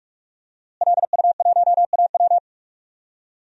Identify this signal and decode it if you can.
Morse code, heard as a keyed tone, about 44 wpm, 700 Hz. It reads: LU1AW
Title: DXCC-100 Call Signs - 5 Characters - Repeated 2x - 45wpm